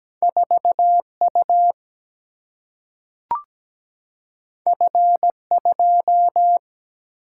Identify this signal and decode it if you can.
Morse code, heard as a keyed tone, about 17 wpm, 700 Hz. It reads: 4U F2
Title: Single-Letter Single-Number 17wpm